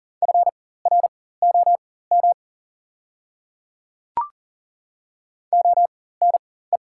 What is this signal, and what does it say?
Morse code, heard as a keyed tone, about 40 wpm, spaced as for 19 wpm, 690 Hz. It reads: FROM ONE
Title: 4x Character Spacing - Top 300 Words - 40wpm